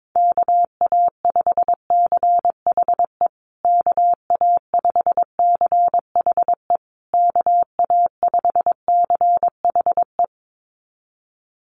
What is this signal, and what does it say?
Morse code, heard as a keyed tone, about 22 wpm, 700 Hz. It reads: KA5CHE KA5CHE KA5CHE
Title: USA Call Signs Repeated 3x 22wpm